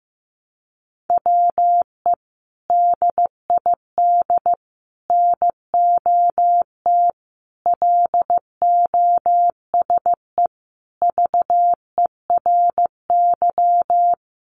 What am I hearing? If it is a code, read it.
Morse code, heard as a keyed tone, about 15 wpm, 700 Hz. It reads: WE DID NOT LOSE VERY
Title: Sentences from Top 1100 Words 15wpm